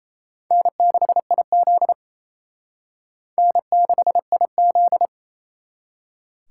Morse code, heard as a keyed tone, about 35 wpm, 700 Hz. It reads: N6IZ N6IZ
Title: Speed Racing - USA 1x2 Call Signs 28wpm